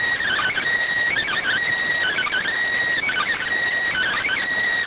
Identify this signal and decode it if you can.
Emergency Alert System (SAME) tones (header bursts), heard as harsh buzzing bursts. It is heard from 0.2 to 4.4 s.